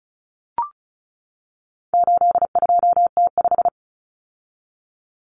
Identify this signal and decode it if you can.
Morse code, heard as a keyed tone, about 34 wpm, 700 Hz. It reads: 82T5
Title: Common Cut Numbers - 4-Digit 35wpm